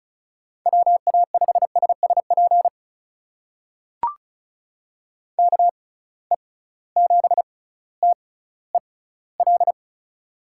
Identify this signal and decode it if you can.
Morse code, heard as a keyed tone, about 36 wpm, spaced as for 18 wpm, 700 Hz. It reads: WA5SSP KE7TEL
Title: USA Call Signs - 6 Characters - 6x Character Spacing - 35wpm